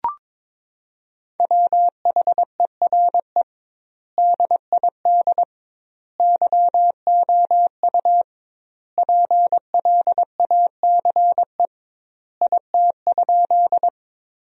Morse code, heard as a keyed tone, about 22 wpm, 700 Hz. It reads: WHERE DID YOU PLACE IT?
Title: 2x Word-Spacing - Sentences from Top 200 Words - 22wpm